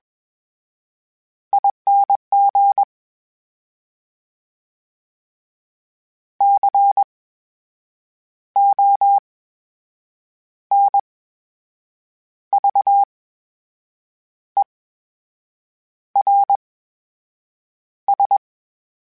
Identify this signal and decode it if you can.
Morse code, heard as a keyed tone, about 21 wpm, spaced as for 6 wpm, 800 Hz. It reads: ING CONVERS
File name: Special INGs-mp3s / Special INGs-mp3s / VerySlowToFaster Ing @20